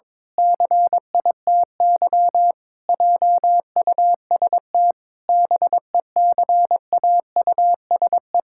Morse code, heard as a keyed tone, about 22 wpm, 700 Hz. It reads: CITY JUST BECAUSE
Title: Pangram Challenge v2 - 22wpm